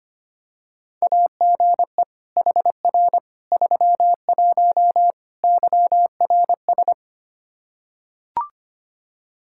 Morse code, heard as a keyed tone, about 25 wpm, 700 Hz. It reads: AGE HR 31 YRS
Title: Practice QSO - Exchange 3 - More Info - 25wpm